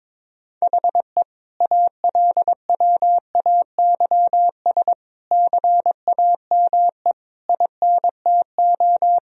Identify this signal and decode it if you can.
Morse code, heard as a keyed tone, about 22 wpm, 700 Hz. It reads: HE ALWAYS CAME INTO
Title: Sentences from Top 1000 Words - Review Part 4 - 22wpm